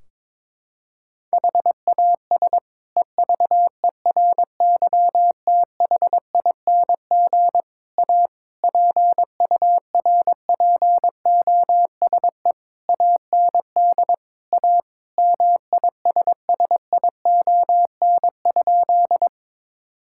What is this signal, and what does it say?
Morse code, heard as a keyed tone, about 22 wpm, 700 Hz. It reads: HAS EVERYTHING A PURPOSE AND A MISSION?
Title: Sentences from Top 1600 Words 22wpm